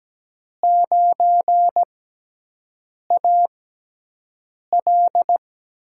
Morse code, heard as a keyed tone, about 17 wpm, spaced as for 6 wpm, 700 Hz. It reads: 9AL
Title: USA Call Signs - 5 Characters - 6x Character Spacing - 17wpm